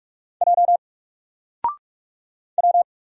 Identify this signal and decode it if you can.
Morse code, heard as a keyed tone, about 44 wpm, 700 Hz. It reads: J W
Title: Single Letter-Number 45wpm